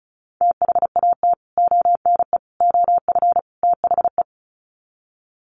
Morse code, heard as a keyed tone, about 34 wpm, 700 Hz. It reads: THAT ONE OF THE
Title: Sets of 4 Words 35wpm